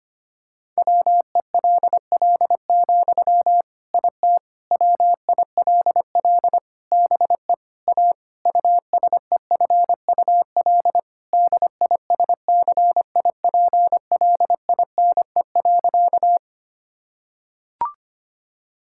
Morse code, heard as a keyed tone, about 25 wpm, 700 Hz. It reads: WELL, IT WILL BE A USEFUL DISCIPLINE.
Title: Sentences from Top 1000 Words - Review Part 6 - 25wpm